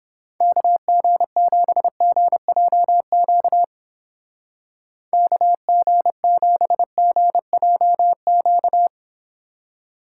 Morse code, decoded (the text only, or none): KG7GJQ KG7GJQ